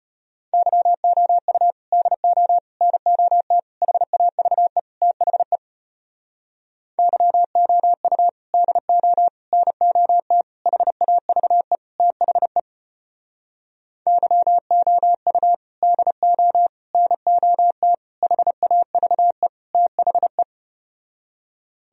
Morse code, heard as a keyed tone, about 33 wpm, 700 Hz. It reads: YOU DO NOT HAVE THE YOU DO NOT HAVE THE YOU DO NOT HAVE THE